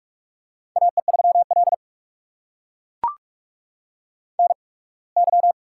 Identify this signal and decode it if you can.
Morse code, heard as a keyed tone, about 44 wpm, spaced as for 27 wpm, 690 Hz. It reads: AE3L NY
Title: USA Call Signs - 4 Characters - 8x Character Spacing - 45wpm